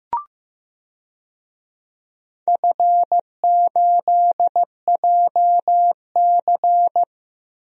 Morse code, heard as a keyed tone, about 15 wpm, 700 Hz. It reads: F8JC
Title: DXCC-100 Call Signs - 4 Characters - Repeated 2x - 15wpm